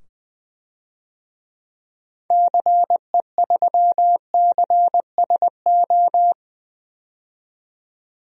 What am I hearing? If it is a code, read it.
Morse code, heard as a keyed tone, about 20 wpm, 700 Hz. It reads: CE3CSO
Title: DXCC-100 Call Signs - 6 Characters - 20wpm